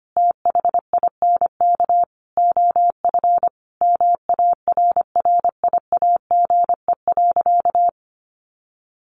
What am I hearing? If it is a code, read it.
Morse code, heard as a keyed tone, about 25 wpm, 700 Hz. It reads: THINK OF MARRIAGE.
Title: Sentences from Top 800 Words 25wpm